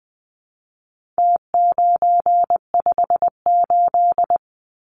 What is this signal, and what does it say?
Morse code, heard as a keyed tone, about 20 wpm, 700 Hz. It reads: T958
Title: Numbers - Contest Style - 4-Digit 20wpm